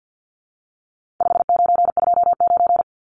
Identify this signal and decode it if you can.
Morse code, heard as a keyed tone, about 50 wpm, 700 Hz. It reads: 5929
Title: Numbers - 4-Digit 50wpm